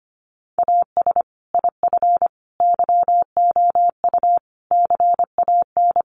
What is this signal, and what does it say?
Morse code, heard as a keyed tone, about 25 wpm, 700 Hz. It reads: AS IF YOU CAN
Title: Sentences from Top 1300 Words 25wpm